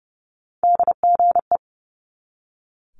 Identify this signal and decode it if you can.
Morse code, heard as a keyed tone, about 30 wpm, 690 Hz. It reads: DGE
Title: Multi-letter Phonemes 30wpm